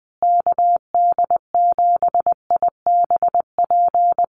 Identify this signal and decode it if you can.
Morse code, heard as a keyed tone, about 20 wpm, 700 Hz. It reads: KD7IBP
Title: USA Call Signs Repeated 3x 20wpm